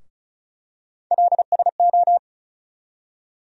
Morse code, heard as a keyed tone, about 36 wpm, 700 Hz. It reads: LSO
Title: USA Call Sign Postfixes - 3 Letters 35wpm